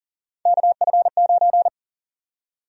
Morse code, heard as a keyed tone, about 40 wpm, 700 Hz. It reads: KF9
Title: USA Call Sign Prefixes - 40wpm